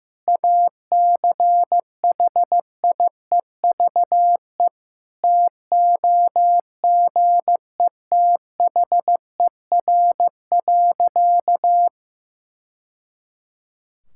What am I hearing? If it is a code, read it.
Morse code, heard as a keyed tone, about 16 wpm, 700 Hz. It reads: ACHIEVE TOGETHER.
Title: Sentences from Top 1000 Words - Review Part 6 - 15wpm